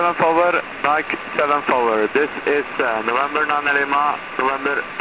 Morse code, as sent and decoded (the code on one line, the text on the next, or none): none